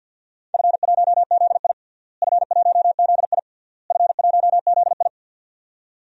Morse code, heard as a keyed tone, about 52 wpm, 700 Hz. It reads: F1ZI F1ZI F1ZI
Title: DXCC-100 Call Signs - 4 Characters - Repeated 3x - 50wpm